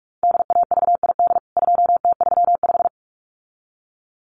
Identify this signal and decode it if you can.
Morse code, heard as a keyed tone, about 44 wpm, 700 Hz. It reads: DAVID 2T35
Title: CWT Exchange 45wpm